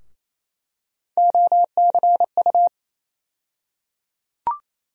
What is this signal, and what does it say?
Morse code, heard as a keyed tone, about 28 wpm, 700 Hz. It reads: OCU
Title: USA Call Sign Postfixes - 3 Letters 28wpm